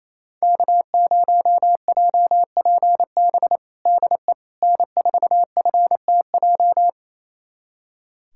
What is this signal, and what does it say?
Morse code, heard as a keyed tone, about 27 wpm, 700 Hz. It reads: K0JPB DE N4FTJ
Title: USA Callsign DE Callsign - 5 Characters - 28wpm